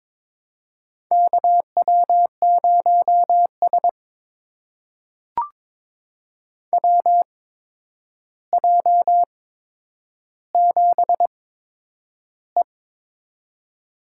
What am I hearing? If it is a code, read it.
Morse code, heard as a keyed tone, about 22 wpm, spaced as for 9 wpm, 700 Hz. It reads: KW0S WJ7E
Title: USA Call Signs - 8x Character Spacing - 22wpm